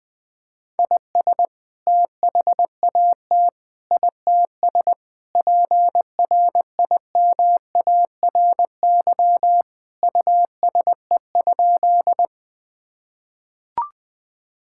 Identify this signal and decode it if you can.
Morse code, heard as a keyed tone, about 20 wpm, 700 Hz. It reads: IS THAT ITS PRIMARY USE?
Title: Sentences from Top 1000 Words - Review Part 5 - 20wpm